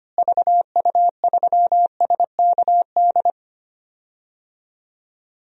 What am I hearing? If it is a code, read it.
Morse code, heard as a keyed tone, about 25 wpm, 700 Hz. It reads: VU3SKD
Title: DXCC-100 Call Signs - 6 Characters - Repeated 2x - 25wpm